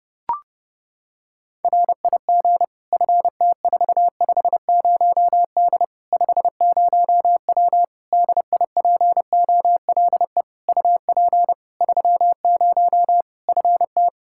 Morse code, heard as a keyed tone, about 30 wpm, 700 Hz. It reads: RIG FT450D 50W DIPOLE UP 30 FT
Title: Practice QSO - Exchange 2 - Info - 30wpm